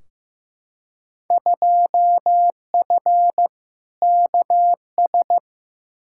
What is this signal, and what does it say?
Morse code, heard as a keyed tone, about 15 wpm, 700 Hz. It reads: 2F KS
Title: Field Day Exchanges - 15wpm